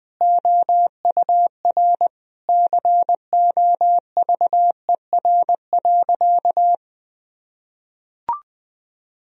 Morse code, heard as a keyed tone, about 20 wpm, 700 Hz. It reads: OUR COVER.
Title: Sentences from Top 1000 Words - Review Part 1 - 20wpm